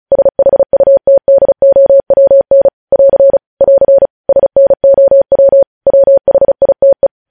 Morse code, heard as a keyed tone, about 36 wpm, 550 Hz. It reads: SHUTDOWN <AR> <AR> SNOW WHITE
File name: podcast / News headlines-1x-35x35